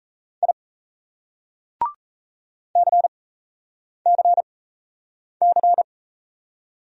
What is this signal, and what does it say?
Morse code, heard as a keyed tone, about 35 wpm, spaced as for 8 wpm, 700 Hz. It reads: I CCC